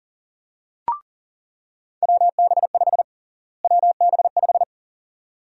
Morse code, heard as a keyed tone, about 40 wpm, 700 Hz. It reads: WB5 WB5
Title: USA Call Sign Prefixes - Repeated 2x - 40wpm